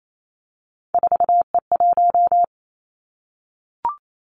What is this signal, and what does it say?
Morse code, heard as a keyed tone, about 28 wpm, 700 Hz. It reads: 4E1
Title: Three Challenging Letter-Number Combinations 28wpm